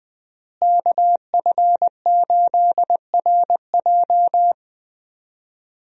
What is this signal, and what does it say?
Morse code, heard as a keyed tone, about 20 wpm, 700 Hz. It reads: KF8RJ
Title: USA Call Signs - 5 Characters - Repeated 3x - 20wpm